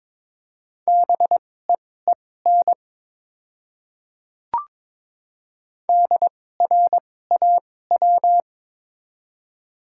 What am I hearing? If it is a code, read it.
Morse code, heard as a keyed tone, about 22 wpm, spaced as for 16 wpm, 700 Hz. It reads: BEEN DRAW
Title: 2x Character Spacing - Top 300 Words - 22wpm